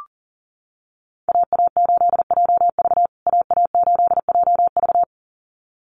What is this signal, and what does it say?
Morse code, heard as a keyed tone, about 40 wpm, 700 Hz. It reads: AA8JV AA8JV
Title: USA Call Signs Repeated 2x 40wpm